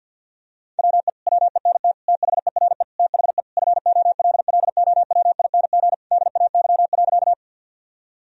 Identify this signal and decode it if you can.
Morse code, heard as a keyed tone, about 50 wpm, 700 Hz. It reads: WE WENT THERE THE FOLLOWING DAY.
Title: Sentences from Top 1000 Words - Review Part 4 - 50wpm